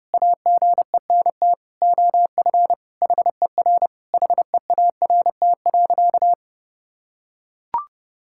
Morse code, heard as a keyed tone, about 30 wpm, 700 Hz. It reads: AGENT OF HER HEART.
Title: Sentences from Top 1600 Words 30wpm